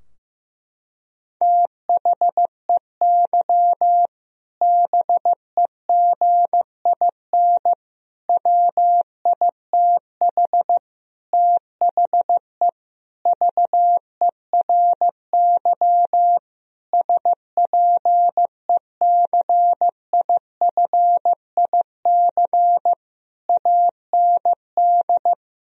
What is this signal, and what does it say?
Morse code, heard as a keyed tone, about 15 wpm, 700 Hz. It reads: THEY BEGIN WITH THE VERY SPECIFIC AND
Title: Sentences from Top 1000 Words - Review Part 3 - 15wpm